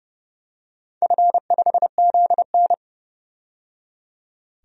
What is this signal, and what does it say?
Morse code, heard as a keyed tone, about 30 wpm, 700 Hz. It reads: F5ZN
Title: DXCC-100 Call Signs - 4 Characters - 30wpm